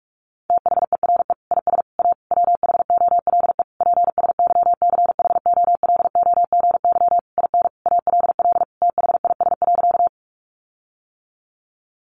Morse code, heard as a keyed tone, about 44 wpm, 700 Hz. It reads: THERE IS A WHOLE PSYCHOLOGY IN ALL THIS.
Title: Sentences from Top 1400 Words 45wpm